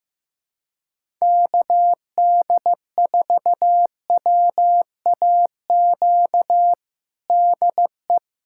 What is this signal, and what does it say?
Morse code, heard as a keyed tone, about 15 wpm, 700 Hz. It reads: KD4WAQ DE
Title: USA Callsign DE Callsign 15wpm